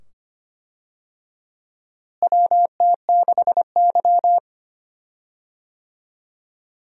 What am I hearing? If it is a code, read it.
Morse code, heard as a keyed tone, about 25 wpm, 700 Hz. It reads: WT6Y
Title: USA Call Signs - 4 Characters - Repeated 2x - 25wpm